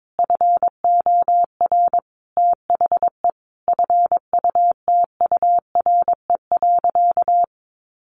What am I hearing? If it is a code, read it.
Morse code, heard as a keyed tone, about 22 wpm, 700 Hz. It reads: FOR THE FUTURE.
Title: Sentences from Top 1400 Words 22wpm